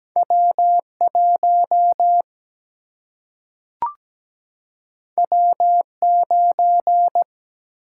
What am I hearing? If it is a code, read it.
Morse code, heard as a keyed tone, about 17 wpm, 700 Hz. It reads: W 1 W 9
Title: Single-Letter Single-Number 17wpm